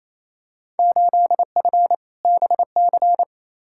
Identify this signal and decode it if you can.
Morse code, heard as a keyed tone, about 28 wpm, 700 Hz. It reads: 8F BC
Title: Field Day Exchanges Encore - 28wpm